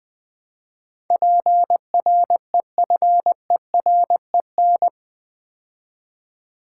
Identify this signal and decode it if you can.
Morse code, heard as a keyed tone, about 20 wpm, 700 Hz. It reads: PREFEREN
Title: Build A Word - 4 Letters At A Time - 10-Letter Words - 20wpm